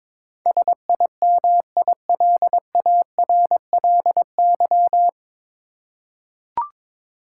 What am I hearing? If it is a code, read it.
Morse code, heard as a keyed tone, about 22 wpm, 700 Hz. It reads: SIMILARLY
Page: Morse Code Ninja practice 22wpm